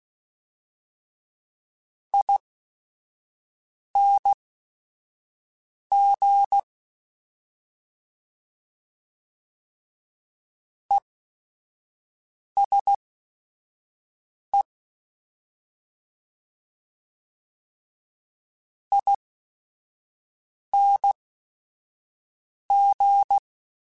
Morse code, heard as a keyed tone, about 16 wpm, spaced as for 5 wpm, 800 Hz. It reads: ING ESE ING
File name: Short Runs / Short Runs-mp3s / ESE ING @5